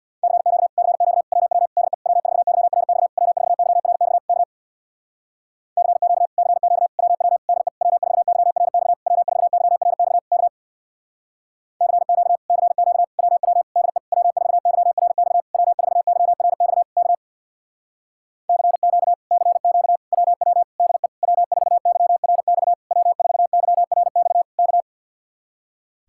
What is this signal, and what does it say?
Morse code, heard as a keyed tone, about 62 wpm, 700 Hz. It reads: CQ CQ WW DE W4YRX W4YRX K CQ CQ WW DE W4YRX W4YRX K CQ CQ WW DE W4YRX W4YRX K CQ CQ WW DE W4YRX W4YRX K
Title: Speed-Racing Common QSO Phrases - 50wpm